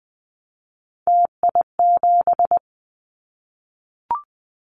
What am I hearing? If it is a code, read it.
Morse code, heard as a keyed tone, about 20 wpm, 700 Hz. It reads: TI7
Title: Callsign Prefixes for Top 100 DXCC - 20wpm